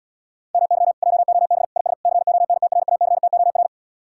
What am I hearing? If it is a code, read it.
Morse code, heard as a keyed tone, about 77 wpm, 700 Hz. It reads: MY JOB IS CONTRACTOR